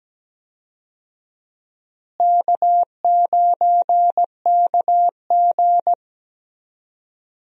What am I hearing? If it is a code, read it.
Morse code, heard as a keyed tone, about 17 wpm, 700 Hz. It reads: K9KG
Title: USA Call Signs - 4 Characters - Repeated 3x - 17wpm